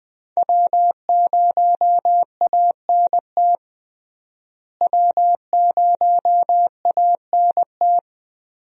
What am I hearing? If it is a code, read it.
Morse code, heard as a keyed tone, about 20 wpm, 700 Hz. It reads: W0ANT W0ANT
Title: USA Call Signs - 5 Characters - Repeated 2x - 20wpm